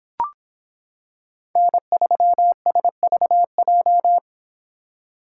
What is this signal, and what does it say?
Morse code, heard as a keyed tone, about 26 wpm, 700 Hz. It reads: N3SVJ